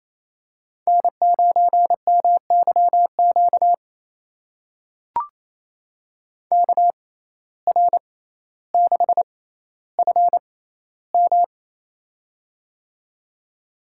Morse code, heard as a keyed tone, about 28 wpm, spaced as for 14 wpm, 700 Hz. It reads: N9MYQ KR6FM
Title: USA Call Signs - 5 Characters - 6x Character Spacing - 28wpm